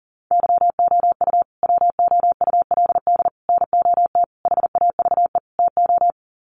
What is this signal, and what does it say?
Morse code, heard as a keyed tone, about 40 wpm, 700 Hz. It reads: YOU WOULD NOT HAVE TO
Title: Sets of 5 Words Encore 40wpm